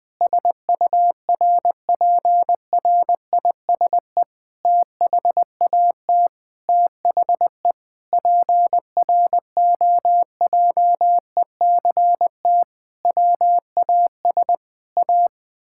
Morse code, heard as a keyed tone, about 20 wpm, 700 Hz. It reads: SURPRISE THAT THE PROJECT WAS A
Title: Sentences with 8-Word Phrases - 20wpm